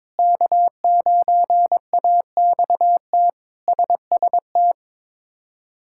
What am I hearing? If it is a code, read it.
Morse code, heard as a keyed tone, about 22 wpm, 700 Hz. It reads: K9AXT SST
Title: K1USN SST QSO Practice - 22wpm